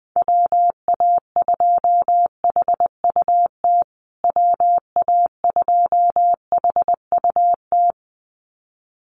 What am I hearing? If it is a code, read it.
Morse code, heard as a keyed tone, about 20 wpm, 700 Hz. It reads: WA2HUT WA2HUT
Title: USA Call Signs Repeated 2x 20wpm